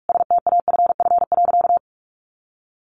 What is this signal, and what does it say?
Morse code, heard as a keyed tone, about 44 wpm, 700 Hz. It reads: STAFF.
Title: Sentences from Top 1200 Words 45wpm